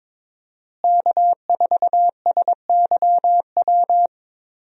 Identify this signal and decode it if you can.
Morse code, heard as a keyed tone, about 22 wpm, 700 Hz. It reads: K4SYW
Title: USA Call Signs - 5 Characters - Repeated 3x - 22wpm